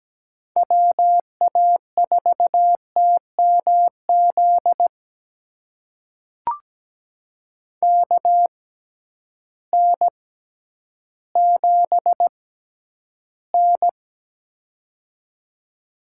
Morse code, heard as a keyed tone, about 17 wpm, spaced as for 10 wpm, 700 Hz. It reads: WA4TMZ KN7N
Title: USA Call Signs - 6x Character Spacing - 17wpm